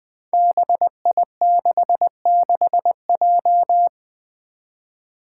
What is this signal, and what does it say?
Morse code, heard as a keyed tone, about 20 wpm, 700 Hz. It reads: BI66J
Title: Five Challenging Letter-Number Combinations 20wpm